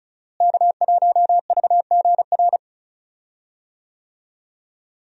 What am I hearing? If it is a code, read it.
Morse code, heard as a keyed tone, about 36 wpm, 700 Hz. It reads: K1VGR
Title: USA Call Signs - 5 Characters - Repeated 2x - 35wpm